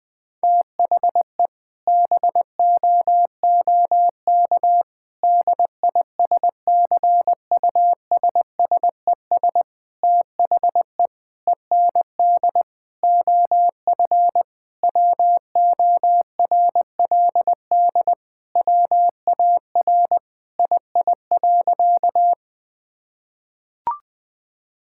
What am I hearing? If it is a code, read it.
Morse code, heard as a keyed tone, about 20 wpm, 700 Hz. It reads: THE BOOK DISCUSSES THE END OF WORLD WAR II.
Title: Sentences with 6-Word Phrases - 20wpm